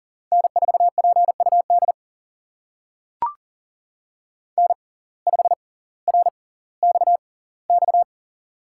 Morse code, heard as a keyed tone, about 39 wpm, spaced as for 20 wpm, 700 Hz. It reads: N4PUD N5RXX
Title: USA Call Signs - 5 Characters - 6x Character Spacing - 40wpm